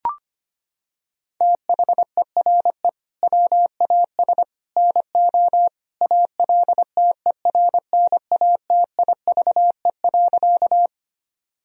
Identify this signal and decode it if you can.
Morse code, heard as a keyed tone, about 25 wpm, 700 Hz. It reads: THERE WAS NO ALTERNATIVE.
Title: Sentences from Top 1000 Words - Review Part 4 - 25wpm